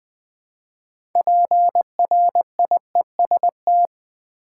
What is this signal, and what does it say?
Morse code, heard as a keyed tone, about 20 wpm, 700 Hz. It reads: PRIEST
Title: 6 Letter Words - First 3 Letters Given - 20wpm